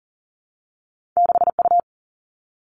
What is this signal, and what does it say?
Morse code, heard as a keyed tone, about 40 wpm, 700 Hz. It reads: 6U
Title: Two Challenging Letter-Number Combinations 40wpm